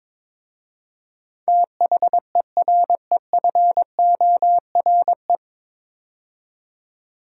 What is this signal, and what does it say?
Morse code, heard as a keyed tone, about 22 wpm, 700 Hz. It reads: THEREFORE
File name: 9 Letter Words - First 3 Letters Given - 22wpm